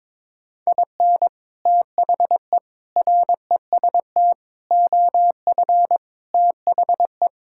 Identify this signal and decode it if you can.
Morse code, heard as a keyed tone, about 22 wpm, 700 Hz. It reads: IN THE REST OF THE
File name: practice / 22wpm / Build A Phrase - 5-Word Phrases - 22wpm